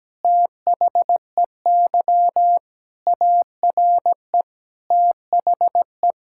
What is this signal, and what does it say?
Morse code, heard as a keyed tone, about 17 wpm, 700 Hz. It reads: THEY ARE THE
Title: Sentences from Top 1000 Words - Review Part 6 - 17wpm